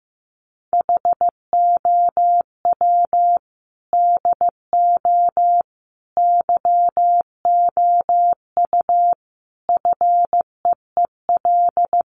Morse code, heard as a keyed tone, about 15 wpm, 700 Hz. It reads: HOW DO YOU FEEL
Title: Sets of 5 Words 15wpm